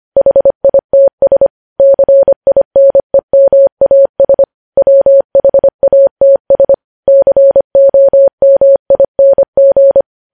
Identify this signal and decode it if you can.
Morse code, heard as a keyed tone, about 25 wpm, 550 Hz. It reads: HITS CINEMAS WHATS COMING
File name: podcast / News headlines-1x-25x25